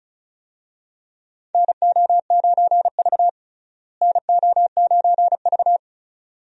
Morse code, heard as a keyed tone, about 36 wpm, 700 Hz. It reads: NO9V NO9V
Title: USA Call Signs - 4 Characters - Repeated 2x - 35wpm